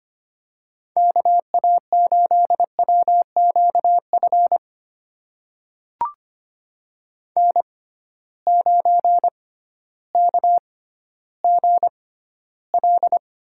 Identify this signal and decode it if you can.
Morse code, heard as a keyed tone, about 25 wpm, spaced as for 13 wpm, 700 Hz. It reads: KA8WQF N9KGL
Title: USA Call Signs - 6x Character Spacing - 25wpm